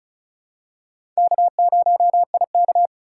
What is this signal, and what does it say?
Morse code, heard as a keyed tone, about 34 wpm, 690 Hz. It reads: K0IK
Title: USA Call Signs - 4x Character Spacing - 35wpm